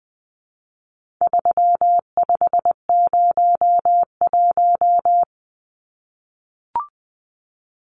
Morse code, heard as a keyed tone, about 20 wpm, 700 Hz. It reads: 3501
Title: Numbers - 4-Digit 20wpm